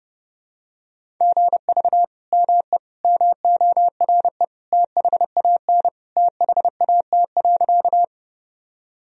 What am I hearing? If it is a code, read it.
Morse code, heard as a keyed tone, about 30 wpm, 700 Hz. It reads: GV ME MORE THAN THAT.
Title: Sentences from Top 200 Words and Abbreviations 30wpm